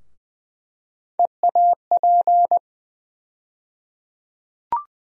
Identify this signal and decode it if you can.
Morse code, heard as a keyed tone, about 21 wpm, 700 Hz. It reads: EAP
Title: USA Call Sign Postfixes - 3 Letters 20wpm